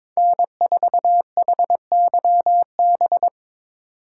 Morse code, heard as a keyed tone, about 26 wpm, 700 Hz. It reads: N4HYB
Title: Speed Racing - USA 1x3 Call Signs 22wpm